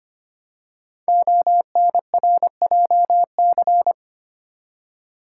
Morse code, heard as a keyed tone, about 25 wpm, 700 Hz. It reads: ONRJC
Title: Code Groups - 5 Letters - 25wpm